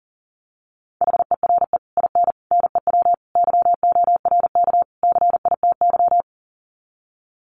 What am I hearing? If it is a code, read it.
Morse code, heard as a keyed tone, about 40 wpm, 700 Hz. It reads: HERE IN NEW YORK CITY
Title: Sets of 5 Words Encore2 40wpm